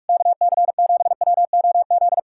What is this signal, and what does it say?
Morse code, heard as a keyed tone, about 59 wpm, spaced as for 52 wpm, 700 Hz. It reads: KC7WOZ
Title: Speed-Racing Common QSO Phrases - 45wpm